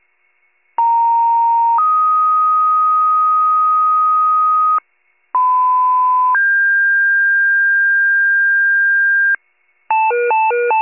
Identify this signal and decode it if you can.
Two-tone page, heard as a one-second tone, then a three-second tone, about 930 then 1250 Hz, about 1010 then 1600 Hz, with noise. It is heard from 0.8 to 9.4 s.
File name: pager / 82 FIRE / wcalert